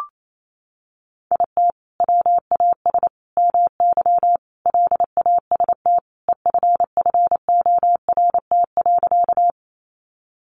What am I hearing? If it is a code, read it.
Morse code, heard as a keyed tone, about 28 wpm, 700 Hz. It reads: IT WAS MY LAST EFFORT.
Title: Sentences from Top 1200 Words 28wpm